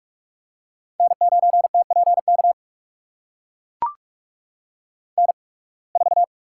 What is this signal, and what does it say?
Morse code, heard as a keyed tone, about 44 wpm, spaced as for 29 wpm, 700 Hz. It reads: N9TPK N4
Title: USA Call Signs - 5 Characters - 8x Character Spacing - 45wpm